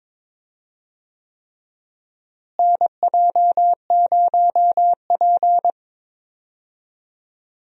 Morse code, heard as a keyed tone, about 22 wpm, 700 Hz. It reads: NJ0P
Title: USA Call Signs - 4 Characters - Repeated 3x - 22wpm